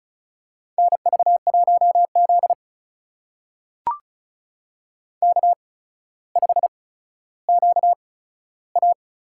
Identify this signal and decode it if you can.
Morse code, heard as a keyed tone, about 36 wpm, spaced as for 15 wpm, 700 Hz. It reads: NV1Z K5QA
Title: USA Call Signs - 8x Character Spacing - 35wpm